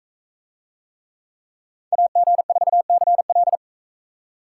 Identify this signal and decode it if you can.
Morse code, heard as a keyed tone, about 44 wpm, 700 Hz. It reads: AG4CL